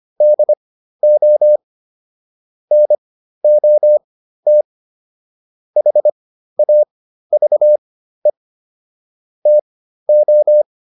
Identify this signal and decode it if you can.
Morse code, heard as a keyed tone, about 25 wpm, spaced as for 13 wpm, 600 Hz. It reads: DO NOT HAVE TO